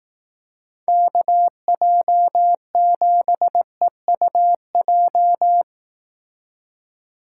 Morse code, heard as a keyed tone, about 18 wpm, 700 Hz. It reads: KJ7EUJ
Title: Speed Racing - USA 2x3 Call Signs 15wpm